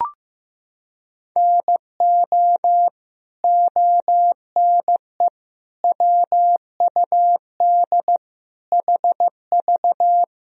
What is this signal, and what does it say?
Morse code, heard as a keyed tone, about 15 wpm, 700 Hz. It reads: NO ONE WUD HV
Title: Sentences from Top 300 Words and Abbreviations 15wpm